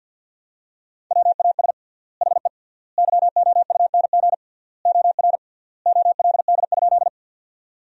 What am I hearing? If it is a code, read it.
Morse code, heard as a keyed tone, about 48 wpm, 700 Hz. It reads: WAS HE YOUNG OR OLD?
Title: 3x Word-Spacing - Sentences from Top 400 Words - 50wpm